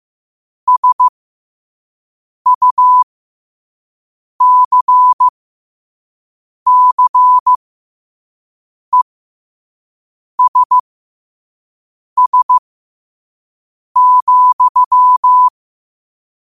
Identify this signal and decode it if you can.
Morse code, heard as a keyed tone, about 15 wpm, spaced as for 5 wpm, 1000 Hz. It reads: SUCCESS,
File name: qotdfiles / 5wpm